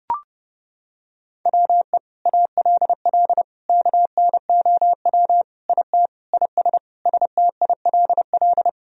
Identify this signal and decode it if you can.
Morse code, heard as a keyed tone, about 32 wpm, 700 Hz. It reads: WE ALL KNOW IT IS STILL
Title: Sentences from Top 1000 Words - Review Part 1 - 30wpm